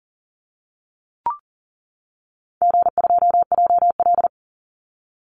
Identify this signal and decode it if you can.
Morse code, heard as a keyed tone, about 40 wpm, 700 Hz. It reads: G2JL
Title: Easy Call Signs - 40wpm